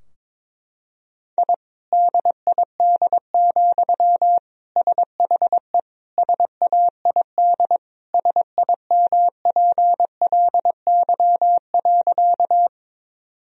Morse code, heard as a keyed tone, about 21 wpm, 700 Hz. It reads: I DID, SHE SAID SIMPLY.
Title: Sentences from Top 1000 Words - Review Part 4 - 22wpm